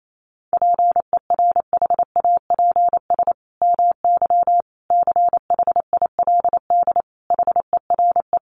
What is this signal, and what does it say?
Morse code, heard as a keyed tone, about 28 wpm, 700 Hz. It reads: PERHAPS MY CHILD HERE
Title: Sentences from Top 1100 Words 28wpm